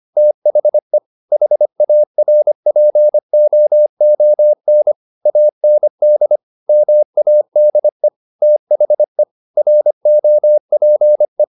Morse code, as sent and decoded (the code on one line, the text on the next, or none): - .... / .... .- .-. .--. --- --- -. / .- -. -.. / -- .- -.. / - .... / .-. --- .--. .
THE HARPOON AND MADE THE ROPE